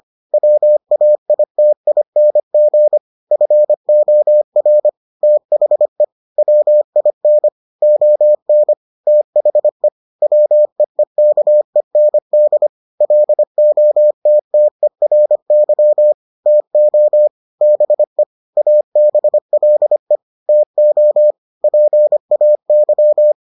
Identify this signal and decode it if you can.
Morse code, heard as a keyed tone, about 25 wpm, 600 Hz. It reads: WAITING FOR THE WIN ON THE WEEKEND LOTTERY TO BE ABLE TO PAY